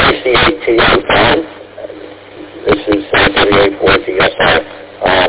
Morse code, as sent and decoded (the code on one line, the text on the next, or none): none